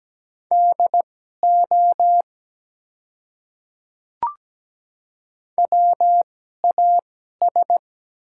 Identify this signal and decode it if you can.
Morse code, heard as a keyed tone, about 17 wpm, spaced as for 12 wpm, 700 Hz. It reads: DO WAS
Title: 2x Character Spacing - Top 100 Words - 17wpm